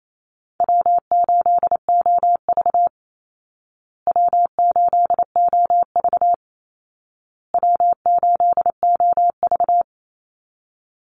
Morse code, heard as a keyed tone, about 28 wpm, 700 Hz. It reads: W8OV W8OV W8OV
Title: Easy Call Signs Repeated 3x - 28wpm